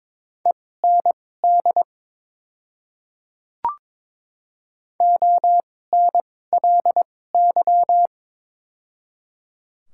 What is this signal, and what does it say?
Morse code, heard as a keyed tone, about 21 wpm, spaced as for 16 wpm, 700 Hz. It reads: END ONLY
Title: 2x Character Spacing - Top 300 Words - 22wpm